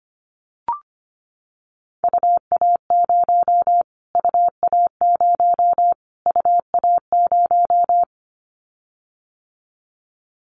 Morse code, heard as a keyed tone, about 25 wpm, 700 Hz. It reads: UA0 UA0 UA0
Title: Callsign Prefixes for Top 25 DXCC - Repeated 3x - 25wpm